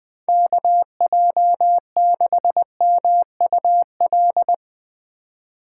Morse code, heard as a keyed tone, about 20 wpm, 700 Hz. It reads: KJ6MUL